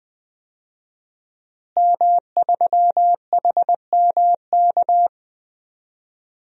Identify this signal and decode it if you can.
Morse code, heard as a keyed tone, about 20 wpm, 700 Hz. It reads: M3HMK
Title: DXCC-100 Call Signs - 5 Characters - Repeated 2x - 20wpm